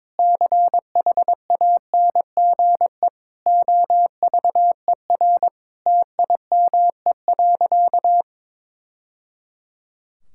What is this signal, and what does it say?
Morse code, heard as a keyed tone, about 22 wpm, 700 Hz. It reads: CHANGE OVER TIME.
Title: Sentences from Top 1000 Words - Review Part 6 - 22wpm